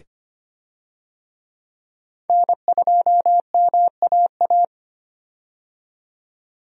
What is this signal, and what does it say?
Morse code, heard as a keyed tone, about 25 wpm, 690 Hz. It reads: N2MAA
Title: USA Call Signs - 5 Characters - 25wpm